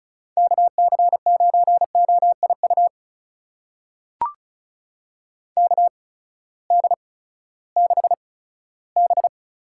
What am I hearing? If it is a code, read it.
Morse code, heard as a keyed tone, about 36 wpm, spaced as for 18 wpm, 700 Hz. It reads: KC9OIU KD6B
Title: USA Call Signs - 6 Characters - 8x Character Spacing - 35wpm